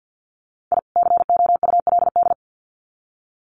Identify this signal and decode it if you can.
Morse code, heard as a keyed tone, about 50 wpm, 700 Hz. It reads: I COULD
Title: Sets of 2 Words 50wpm